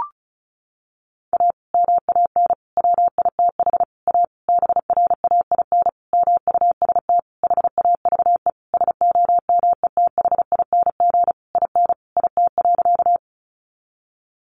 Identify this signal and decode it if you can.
Morse code, heard as a keyed tone, about 34 wpm, 700 Hz. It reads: A MAN WITH A BRAIN MUST HAVE SOMETHING IN IT.
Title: Sentences from Top 1100 Words 35wpm